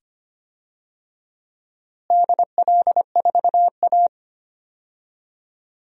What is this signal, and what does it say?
Morse code, heard as a keyed tone, about 25 wpm, 700 Hz. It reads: DL4A
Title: DXCC-100 Call Signs - 4 Characters - Repeated 2x - 25wpm